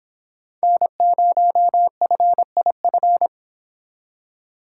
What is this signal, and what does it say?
Morse code, heard as a keyed tone, about 26 wpm, 700 Hz. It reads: N0FIF